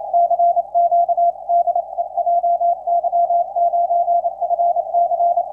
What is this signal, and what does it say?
Morse code, heard as a keyed tone, about 27 wpm, 690 Hz. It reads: CQ DE JY9FC
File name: jy9fc 04-02-17 1947utc 160m